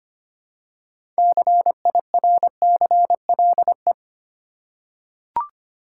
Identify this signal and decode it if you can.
Morse code, heard as a keyed tone, about 25 wpm, 700 Hz. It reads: CIRCLE
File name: practice / 25wpm / Easy Words with Hints - 25wpm